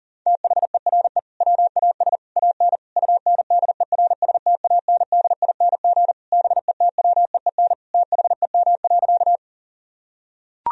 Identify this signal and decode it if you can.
Morse code, heard as a keyed tone, about 40 wpm, 700 Hz. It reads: THERE WAS AN UNDERSTANDING BETWEEN THEM.
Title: Sentences from Top 1000 Words - Review Part 3 - 40wpm